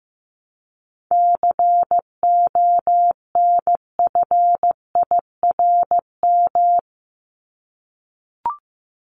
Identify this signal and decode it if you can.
Morse code, heard as a keyed tone, about 15 wpm, 700 Hz. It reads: CONFIRM
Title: Top 1751-2000 Words 15wpm